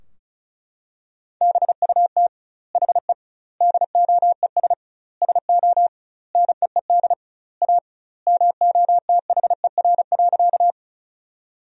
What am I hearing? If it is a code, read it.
Morse code, heard as a keyed tone, about 36 wpm, 700 Hz. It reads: BUT HE DOES SO NEED A MOTHER.
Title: 2x Word-Spacing - Sentences from Top 200 Words - 35wpm